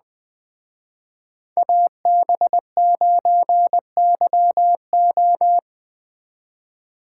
Morse code, heard as a keyed tone, about 20 wpm, 700 Hz. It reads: AB9YO
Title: Speed Racing - USA Call Signs 17wpm